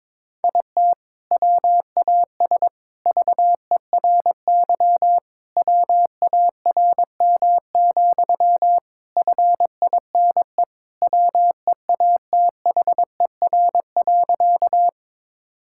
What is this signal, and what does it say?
Morse code, heard as a keyed tone, about 22 wpm, 700 Hz. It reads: IT WAS VERY WARM, FINE WEATHER.
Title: Sentences from Top 1000 Words - Review Part 5 - 22wpm